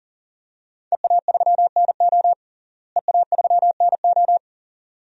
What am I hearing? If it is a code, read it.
Morse code, heard as a keyed tone, about 40 wpm, 700 Hz. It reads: EA3NO EA3NO
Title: DXCC-100 Call Signs - 5 Characters - Repeated 2x - 40wpm